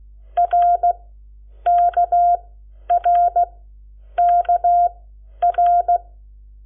Morse code, heard as a keyed tone, about 16 wpm, spaced as for 9 wpm, 700 Hz. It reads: RKRKR
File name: Lecciones / CLASE Nº7